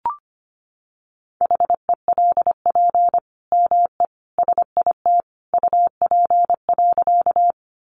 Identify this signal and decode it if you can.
Morse code, heard as a keyed tone, about 25 wpm, 700 Hz. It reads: HELP ME SIT UP.
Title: Sentences from Top 500 Words 25wpm